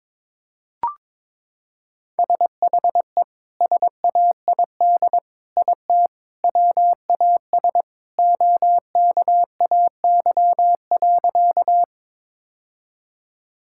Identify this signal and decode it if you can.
Morse code, heard as a keyed tone, about 22 wpm, 700 Hz. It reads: SHE SAID IT WAS OKAY.
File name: practice / 22wpm / Sentences with 2-Word Phrases - 22wpm